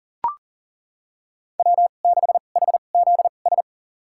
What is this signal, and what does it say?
Morse code, heard as a keyed tone, about 40 wpm, spaced as for 29 wpm, 700 Hz. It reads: W6HZS
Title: USA Call Signs - 5 Characters - 2x Character Spacing - 40wpm